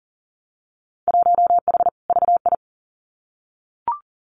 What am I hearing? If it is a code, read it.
Morse code, heard as a keyed tone, about 40 wpm, 700 Hz. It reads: 1H VI
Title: Winter Field Day Exchanges Encore 40wpm